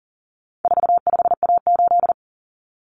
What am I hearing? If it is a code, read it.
Morse code, heard as a keyed tone, about 40 wpm, 700 Hz. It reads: 45A8
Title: Common Cut Numbers - 4-Digit 40wpm